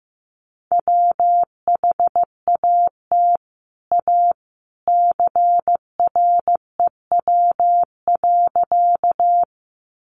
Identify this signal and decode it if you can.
Morse code, heard as a keyed tone, about 15 wpm, 700 Hz. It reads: WHAT A CREW.
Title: Sentences from Top 1500 Words 15wpm